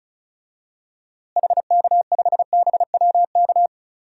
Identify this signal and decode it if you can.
Morse code, heard as a keyed tone, about 36 wpm, 700 Hz. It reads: HK5BWK
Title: DXCC-100 Call Signs - 6 Characters - Repeated 3x - 35wpm